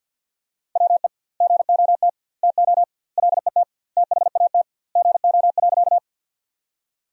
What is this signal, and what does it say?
Morse code, heard as a keyed tone, about 52 wpm, 700 Hz. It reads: WE GOT TO LET THAT GO.
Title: 2x Word-Spacing - Sentences from Top 300 Words - 50wpm